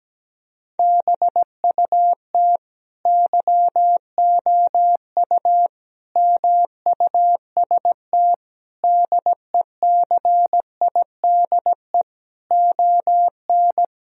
Morse code, heard as a keyed tone, about 17 wpm, 700 Hz. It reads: BUT YOU MUST DECIDE ON
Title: Sentences from Top 1000 Words - Review Part 3 - 17wpm